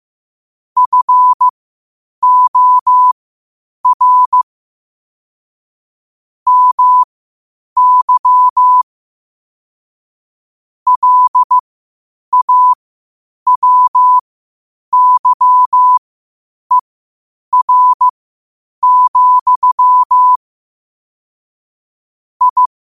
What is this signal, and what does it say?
Morse code, heard as a keyed tone, about 15 wpm, spaced as for 8 wpm, 1000 Hz. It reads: FOR MY LAWYER, I
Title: 8 WPM morse code quotes for Thu, 29 Feb 2024